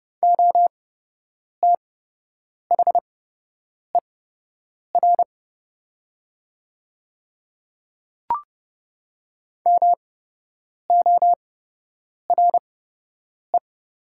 Morse code, heard as a keyed tone, about 29 wpm, spaced as for 8 wpm, 700 Hz. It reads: OTHER MORE
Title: 8x Character Spacing - Top 100 Words - 30wpm